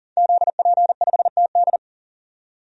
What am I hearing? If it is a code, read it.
Morse code, heard as a keyed tone, about 40 wpm, 700 Hz. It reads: ZP5TD
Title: DXCC-100 Call Signs - 5 Characters - 40wpm